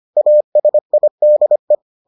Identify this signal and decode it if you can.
Morse code, heard as a keyed tone, about 25 wpm, 600 Hz. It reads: ASIDE